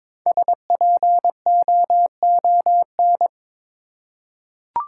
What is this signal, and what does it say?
Morse code, heard as a keyed tone, about 23 wpm, 700 Hz. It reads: SPOON
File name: practice / 22wpm / 5 Letter Words - First 3 Letters Given - 22wpm